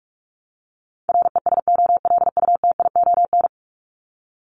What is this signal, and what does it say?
Morse code, heard as a keyed tone, about 44 wpm, 700 Hz. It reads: RESOLUTION
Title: Top 1501-1750 Words 45wpm